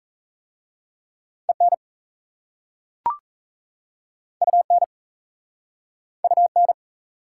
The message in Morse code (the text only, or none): EN UN UN